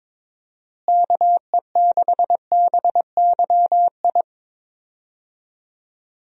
Morse code, heard as a keyed tone, about 22 wpm, 700 Hz. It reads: KE6BYI
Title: USA Call Signs - 6 Characters - Repeated 3x - 22wpm